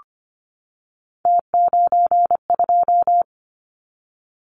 Morse code, heard as a keyed tone, about 25 wpm, 700 Hz. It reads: T92
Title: Callsign Prefixes for Top 100 DXCC - 25wpm